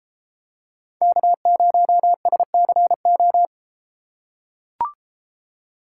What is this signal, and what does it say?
Morse code, heard as a keyed tone, about 32 wpm, 700 Hz. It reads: K0SCO